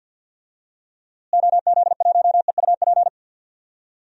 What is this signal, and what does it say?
Morse code, heard as a keyed tone, about 50 wpm, 700 Hz. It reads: OZ1EUP
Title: DXCC-100 Call Signs - 6 Characters - Repeated 2x - 50wpm